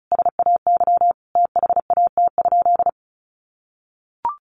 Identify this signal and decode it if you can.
Morse code, heard as a keyed tone, about 34 wpm, 700 Hz. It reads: SAY THAT?
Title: Sentences from Top 1500 Words 35wpm